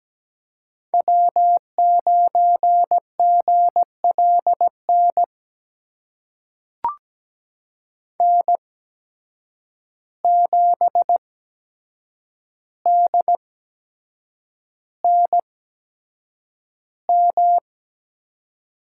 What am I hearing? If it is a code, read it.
Morse code, heard as a keyed tone, about 17 wpm, spaced as for 7 wpm, 700 Hz. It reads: W9GLN N7DNM
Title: USA Call Signs - 5 Characters - 8x Character Spacing - 17wpm